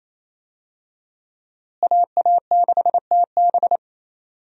Morse code, heard as a keyed tone, about 28 wpm, 700 Hz. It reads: AA6TB
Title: USA Call Signs - 5 Characters - Repeated 2x - 28wpm